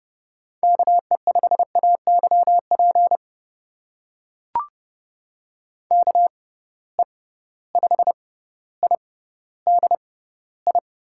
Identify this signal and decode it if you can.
Morse code, heard as a keyed tone, about 30 wpm, spaced as for 15 wpm, 700 Hz. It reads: KE5AYP KE5IDI
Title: USA Call Signs - 6 Characters - 6x Character Spacing - 30wpm